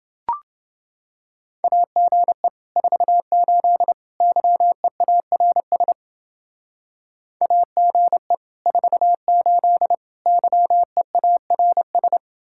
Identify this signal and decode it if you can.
Morse code, heard as a keyed tone, about 28 wpm, 700 Hz. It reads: AGE 48 YEARS AGE 48 YEARS